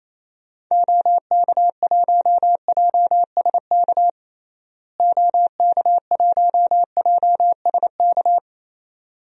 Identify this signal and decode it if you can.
Morse code, heard as a keyed tone, about 28 wpm, 700 Hz. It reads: OK1JSK OK1JSK
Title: DXCC-100 Call Signs - 6 Characters - Repeated 2x - 28wpm